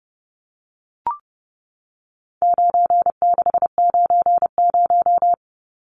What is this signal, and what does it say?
Morse code, heard as a keyed tone, about 30 wpm, 700 Hz. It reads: 9690
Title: Numbers - 4-Digit 30wpm